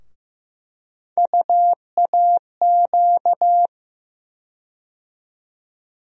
Morse code, heard as a keyed tone, about 15 wpm, 700 Hz. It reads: UAQ
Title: USA Call Sign Postfixes - 3 Letters 15wpm